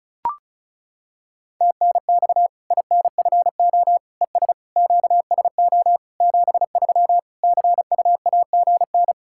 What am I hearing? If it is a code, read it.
Morse code, heard as a keyed tone, about 36 wpm, 700 Hz. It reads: TNX INFO ES QSO 73 CUAGN
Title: Practice QSO - Exchange 4 - Ending - 35wpm